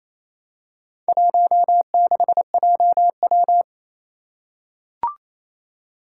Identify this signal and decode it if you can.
Morse code, heard as a keyed tone, about 28 wpm, 700 Hz. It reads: 16JW
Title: Four Challenging Letter-Number Combinations 28wpm